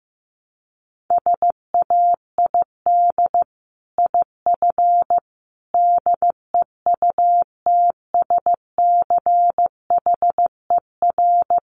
Morse code, heard as a keyed tone, about 15 wpm, 700 Hz. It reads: SAID IF DEUTSCHER
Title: A Sound Of Thunder 15wpm